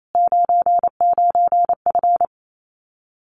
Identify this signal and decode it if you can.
Morse code, heard as a keyed tone, about 28 wpm, 700 Hz. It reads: 99F
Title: Common QSO Phrases Encore 28wpm